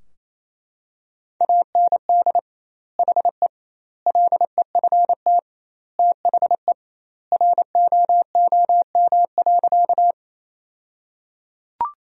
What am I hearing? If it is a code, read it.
Morse code, heard as a keyed tone, about 28 wpm, 700 Hz. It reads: AND HE LEFT THE ROOM.
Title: 2x Word-Spacing - Sentences from Top 300 Words - 28wpm